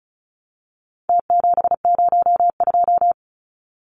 Morse code, heard as a keyed tone, about 34 wpm, 700 Hz. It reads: T702
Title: Numbers - Contest Style - 4-Digit 35wpm